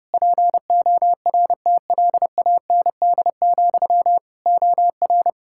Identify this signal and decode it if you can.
Morse code, heard as a keyed tone, about 32 wpm, 700 Hz. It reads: PORTLAND, OR
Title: Practice QSO - Exchange 1 - Beginning - 30wpm